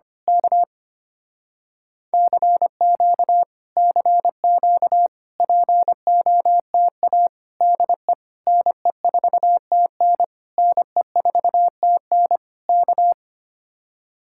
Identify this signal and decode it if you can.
Morse code, heard as a keyed tone, about 24 wpm, 700 Hz. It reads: K CQ CQ POTA DE NE4TN NE4TN K
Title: Speed-Racing Common QSO Phrases - 25wpm